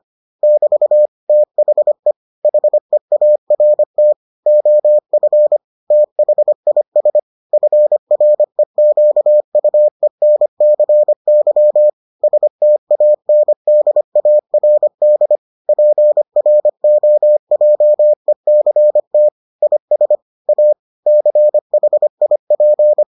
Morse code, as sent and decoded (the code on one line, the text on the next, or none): -...- / - .... / .... .- .-. - / --- ..-. / - .... .. ... / ..-. .-. --.- ..- . -. -.-. -.-- / ... - .- -. -.. .- .-. -.. / .--. .-. --- .--- . -.-. - / .. ... / .- / -.-. .... .. .--.
<BT> THE HEART OF THIS FREQUENCY STANDARD PROJECT IS A CHIP